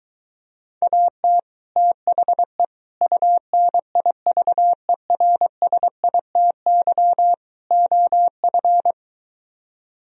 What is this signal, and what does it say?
Morse code, heard as a keyed tone, about 23 wpm, 700 Hz. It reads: AT THE UNIVERSITY OF